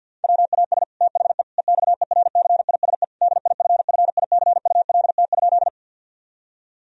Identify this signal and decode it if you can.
Morse code, heard as a keyed tone, about 50 wpm, 700 Hz. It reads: WAS THE EXERCISE DIFFICULT?
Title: Sentences from Top 1000 Words - Review Part 4 - 50wpm